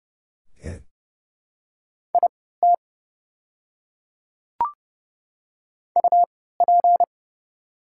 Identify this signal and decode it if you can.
Morse code, heard as a keyed tone, about 31 wpm, spaced as for 17 wpm, 700 Hz. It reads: IT UP
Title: 3x Character Spacing - Top 100 Words - 30wpm